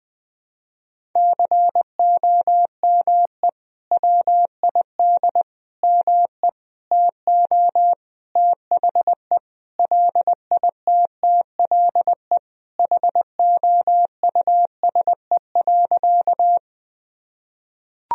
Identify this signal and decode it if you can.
Morse code, heard as a keyed tone, about 20 wpm, 700 Hz. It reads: COME WID ME TO THE LITTLE HOUSE.
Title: Sentences from Top 200 Words and Abbreviations 20wpm